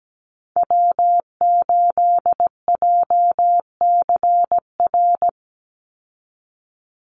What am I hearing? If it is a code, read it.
Morse code, heard as a keyed tone, about 17 wpm, 700 Hz. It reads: W8JCR
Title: USA Call Signs 17wpm